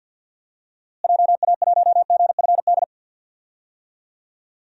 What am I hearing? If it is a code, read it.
Morse code, heard as a keyed tone, about 53 wpm, 700 Hz. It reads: JA1GFD
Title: DXCC-100 Call Signs - 6 Characters - Repeated 3x - 50wpm